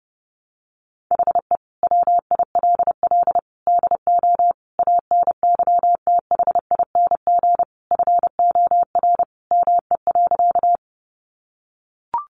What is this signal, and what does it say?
Morse code, heard as a keyed tone, about 30 wpm, 700 Hz. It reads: HE WILL DO ANYTHING FOR ME.
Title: Sentences from Top 1000 Words 30wpm